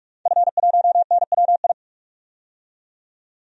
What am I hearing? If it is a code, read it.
Morse code, heard as a keyed tone, about 47 wpm, 700 Hz. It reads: F1NWI
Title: DXCC-100 Call Signs - 5 Characters - 45wpm